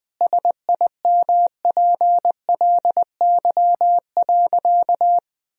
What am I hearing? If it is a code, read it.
Morse code, heard as a keyed tone, about 20 wpm, 700 Hz. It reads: SIMPLY.
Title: Sentences from Top 1000 Words - Review Part 4 - 20wpm